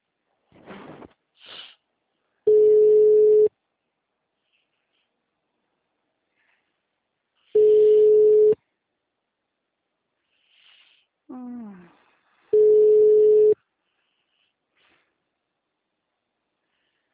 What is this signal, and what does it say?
Ringback tone (Europe, 425 Hz).